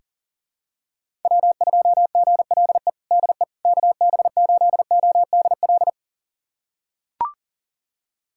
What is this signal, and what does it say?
Morse code, heard as a keyed tone, about 40 wpm, 700 Hz. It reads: W2GLE DE KB8ODL
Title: USA Callsign DE Callsign 40wpm